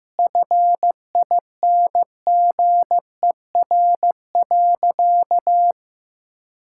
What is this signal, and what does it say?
Morse code, heard as a keyed tone, about 16 wpm, 700 Hz. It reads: FINGER.
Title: Sentences from Top 1600 Words 15wpm